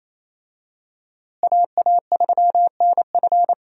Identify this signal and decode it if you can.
Morse code, heard as a keyed tone, about 28 wpm, 700 Hz. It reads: AA3NF
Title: USA Call Signs - 5 Characters - Repeated 3x - 28wpm